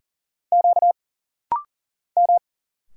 Morse code, heard as a keyed tone, about 40 wpm, 700 Hz. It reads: Q M
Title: Single Letter-Number Alt - Rapid-Fire 40wpm